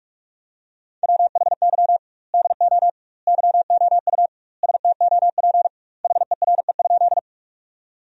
Morse code, heard as a keyed tone, about 42 wpm, 700 Hz. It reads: WHY DO YOU STOP HERE?
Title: 2x Word-Spacing - Sentences from Top 300 Words - 45wpm